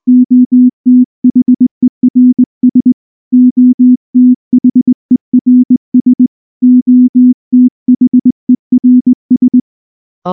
Morse code, heard as a keyed tone, about 19 wpm, 250 Hz. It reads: OTHERS OTHERS OTHERS
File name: morse / morse-training-20WPM-1